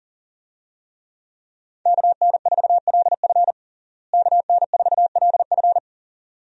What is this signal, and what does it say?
Morse code, heard as a keyed tone, about 40 wpm, 700 Hz. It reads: KN4LF KN4LF
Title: USA Call Signs - 5 Characters - Repeated 2x - 40wpm